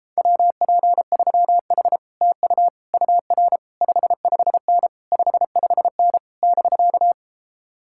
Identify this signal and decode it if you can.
Morse code, heard as a keyed tone, about 33 wpm, 700 Hz. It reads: WP3H TU UR 55N 55N <BK>